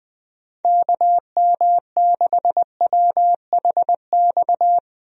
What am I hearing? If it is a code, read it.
Morse code, heard as a keyed tone, about 20 wpm, 700 Hz. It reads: KM6WHX
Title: USA Call Signs - 6 Characters - Repeated 3x - 20wpm